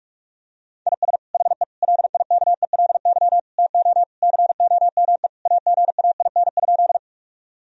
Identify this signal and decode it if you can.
Morse code, heard as a keyed tone, about 46 wpm, 700 Hz. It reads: IS HE LIKELY TO COME AGAIN?
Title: Sentences from Top 1000 Words - Review Part 4 - 45wpm